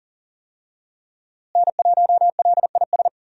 Morse code, heard as a keyed tone, about 40 wpm, 700 Hz. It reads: N1LIS
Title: USA Call Signs - 5 Characters - Repeated 3x - 40wpm